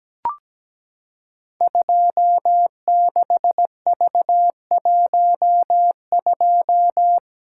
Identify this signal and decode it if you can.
Morse code, heard as a keyed tone, about 17 wpm, 700 Hz. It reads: 26V12
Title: Code Groups - 5 Letter:Numbers - 17wpm